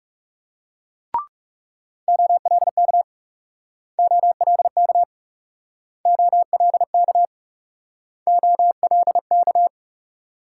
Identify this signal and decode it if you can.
Morse code, heard as a keyed tone, about 38 wpm, 700 Hz. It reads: OLK OLK OLK OLK
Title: Speed-Racing Multi-letter Phonemes - 30wpm